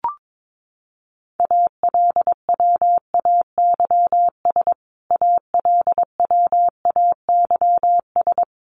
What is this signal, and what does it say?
Morse code, heard as a keyed tone, about 22 wpm, 700 Hz. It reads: ALWAYS ALWAYS
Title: Top 300 Words Repeated 22wpm